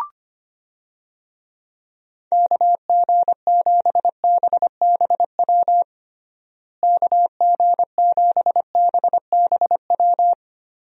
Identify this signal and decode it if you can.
Morse code, heard as a keyed tone, about 25 wpm, 700 Hz. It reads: KG7BBW KG7BBW
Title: USA Call Signs - 6 Characters - Repeated 3x - 25wpm